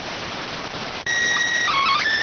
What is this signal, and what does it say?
SSTV (slow-scan television), PD 290.